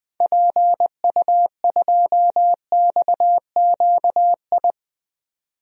Morse code, heard as a keyed tone, about 20 wpm, 700 Hz. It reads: PU2XQI
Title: DXCC-100 Call Signs - 6 Characters - 20wpm